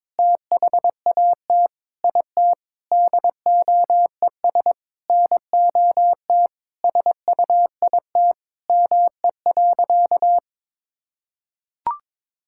Morse code, heard as a keyed tone, about 22 wpm, 700 Hz. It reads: THAT IT DOES NOT SUIT ME.
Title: Sentences from Top 1600 Words 22wpm